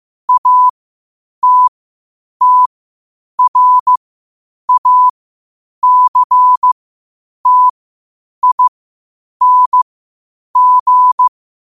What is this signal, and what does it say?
Morse code, heard as a keyed tone, about 15 wpm, spaced as for 8 wpm, 1000 Hz. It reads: ATTRACTING
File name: qotdfiles / 8wpm